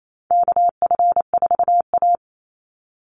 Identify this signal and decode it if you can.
Morse code, heard as a keyed tone, about 28 wpm, 700 Hz. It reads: KF4A
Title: Call Signs Repeated 2x - 28wpm